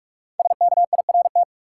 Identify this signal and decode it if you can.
Morse code, heard as a keyed tone, about 46 wpm, 700 Hz. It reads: SKIRT
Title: Five Letter Words - 45wpm